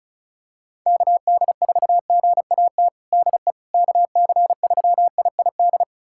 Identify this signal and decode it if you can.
Morse code, heard as a keyed tone, about 36 wpm, 700 Hz. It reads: KD4GAT DE KC3IID
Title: USA Callsign DE Callsign - 6 Characters - 35wpm